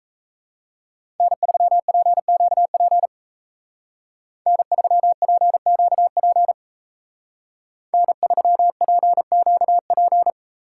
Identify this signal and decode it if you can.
Morse code, heard as a keyed tone, about 36 wpm, 700 Hz. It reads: N3PQP N3PQP N3PQP